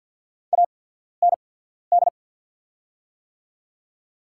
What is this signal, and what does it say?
Morse code, heard as a keyed tone, about 53 wpm, spaced as for 14 wpm, 700 Hz. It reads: AND
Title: 8x Character Spacing - Top 200 Words - 50wpm